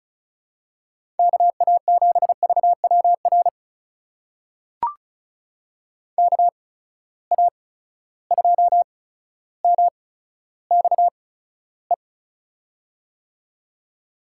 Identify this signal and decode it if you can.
Morse code, heard as a keyed tone, about 36 wpm, spaced as for 15 wpm, 700 Hz. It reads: KA7VWR KA2MXE
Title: USA Call Signs - 6 Characters - 8x Character Spacing - 35wpm